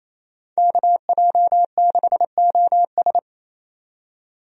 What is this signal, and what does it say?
Morse code, heard as a keyed tone, about 28 wpm, 700 Hz. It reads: KJ6OS
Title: USA Call Signs - 5 Characters - 28wpm